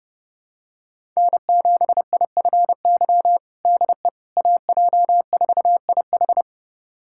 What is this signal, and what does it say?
Morse code, heard as a keyed tone, about 30 wpm, 700 Hz. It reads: N7IFY DE AJ4IH
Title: USA Callsign DE Callsign - 5 Characters - 30wpm